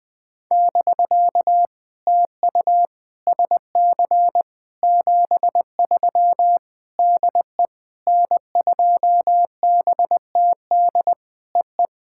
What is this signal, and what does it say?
Morse code, heard as a keyed tone, about 20 wpm, 700 Hz. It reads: <BK> TU SC 73 DE N2BTD EE
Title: POTA QSO Practice - 20wpm